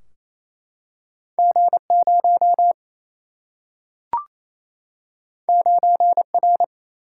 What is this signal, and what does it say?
Morse code, heard as a keyed tone, about 28 wpm, 700 Hz. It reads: G0 9R
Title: Single-Letter Single-Number 28wpm